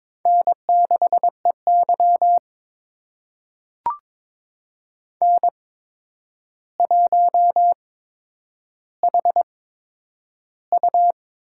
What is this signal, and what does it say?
Morse code, heard as a keyed tone, about 22 wpm, spaced as for 9 wpm, 700 Hz. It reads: N6EY N1HU
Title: USA Call Signs - 4 Characters - 8x Character Spacing - 22wpm